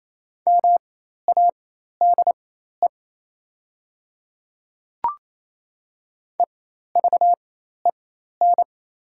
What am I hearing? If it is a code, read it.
Morse code, heard as a keyed tone, about 28 wpm, spaced as for 13 wpm, 700 Hz. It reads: MADE EVEN
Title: 4x Character Spacing - Top 300 Words - 28wpm